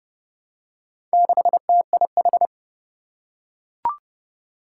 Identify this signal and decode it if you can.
Morse code, heard as a keyed tone, about 30 wpm, 700 Hz. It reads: 6TIH
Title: Four Challenging Letter-Number Combinations 30wpm